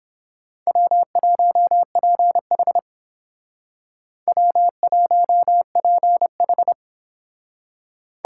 Morse code, heard as a keyed tone, about 31 wpm, 700 Hz. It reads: W1PH W1PH